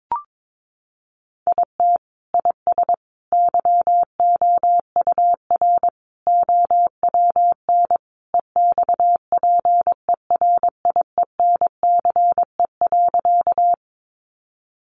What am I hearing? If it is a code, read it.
Morse code, heard as a keyed tone, about 22 wpm, 700 Hz. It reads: IT IS YOUR OWN EXPERIENCE.
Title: Sentences from Top 600 Words 22wpm